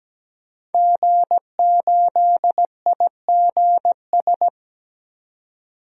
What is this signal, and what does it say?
Morse code, heard as a keyed tone, about 17 wpm, 700 Hz. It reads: G8IGS
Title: DXCC-100 Call Signs - 5 Characters - 17wpm